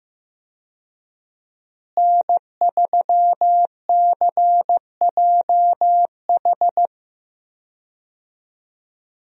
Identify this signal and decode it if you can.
Morse code, heard as a keyed tone, about 16 wpm, 700 Hz. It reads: N3CJH
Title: USA Call Signs - 5 Characters - Repeated 2x - 15wpm